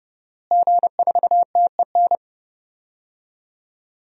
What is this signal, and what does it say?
Morse code, heard as a keyed tone, about 30 wpm, 700 Hz. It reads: G4TEN
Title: DXCC-100 Call Signs - 5 Characters - Repeated 2x - 30wpm